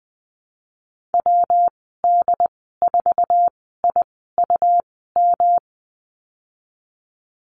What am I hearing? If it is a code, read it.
Morse code, heard as a keyed tone, about 20 wpm, spaced as for 15 wpm, 700 Hz. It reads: WD4IUM
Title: 122 Lesson 20g - Number 3 - Call Signs 20wpm